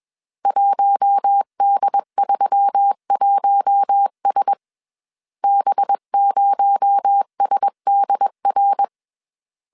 Morse code, heard as a keyed tone, about 21 wpm, 800 Hz. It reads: 1D31S B0SDR
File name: CW / DET / 1 6 1500C S A 20wpm 2048